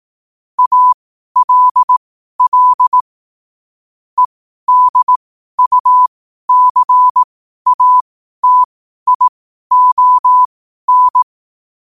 Morse code, heard as a keyed tone, about 18 wpm, spaced as for 12 wpm, 1000 Hz. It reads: ALL EDUCATION
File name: qotdfiles / 12wpm